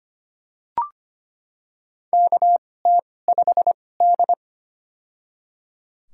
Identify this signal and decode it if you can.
Morse code, heard as a keyed tone, about 25 wpm, spaced as for 18 wpm, 700 Hz. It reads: KT5D
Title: USA Call Signs - 4 Characters - 2x Character Spacing - 25wpm